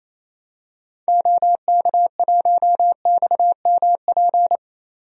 Morse code, heard as a keyed tone, about 28 wpm, 700 Hz. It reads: OK1XMP
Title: DXCC-100 Call Signs - 6 Characters - 28wpm